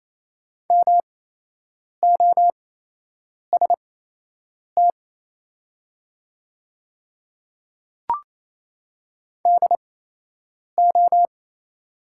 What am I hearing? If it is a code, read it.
Morse code, heard as a keyed tone, about 28 wpm, spaced as for 8 wpm, 700 Hz. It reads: MOST DO
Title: 8x Character Spacing - Top 200 Words - 28wpm